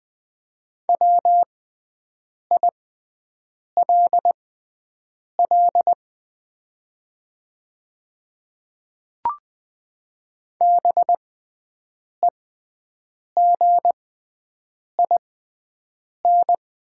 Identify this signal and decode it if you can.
Morse code, heard as a keyed tone, about 20 wpm, spaced as for 7 wpm, 700 Hz. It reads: WILL BEGIN
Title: 6x Character Spacing - Top 300 Words - 20wpm